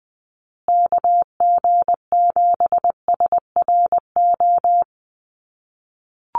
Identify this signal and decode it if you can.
Morse code, heard as a keyed tone, about 20 wpm, 700 Hz. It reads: KG7SRO
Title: USA Call Signs Repeated 3x 20wpm